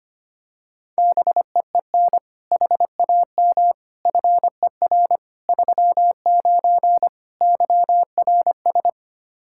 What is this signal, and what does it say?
Morse code, heard as a keyed tone, about 25 wpm, 700 Hz. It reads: BEEN HAM FER 39 YRS
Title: Practice QSO - Exchange 3 - More Info - 25wpm